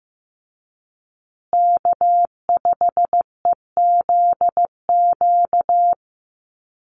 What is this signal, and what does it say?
Morse code, heard as a keyed tone, about 15 wpm, 700 Hz. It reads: K5EZQ
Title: USA Call Signs Repeated 2x 15wpm